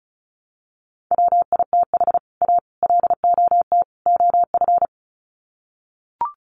Morse code, heard as a keyed tone, about 34 wpm, 700 Hz. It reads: WITH A LOT OF
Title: Sets of 4 Words 35wpm